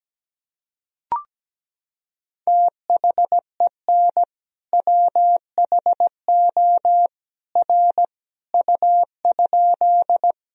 Morse code, heard as a keyed tone, about 17 wpm, 700 Hz. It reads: THEN WHO R U?
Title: Sentences from Top 100 Words and Abbreviations 17wpm